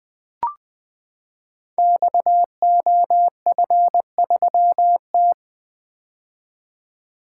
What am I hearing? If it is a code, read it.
Morse code, heard as a keyed tone, about 20 wpm, 700 Hz. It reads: XOF3T
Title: Code Groups - 5 Letter:Numbers - 20wpm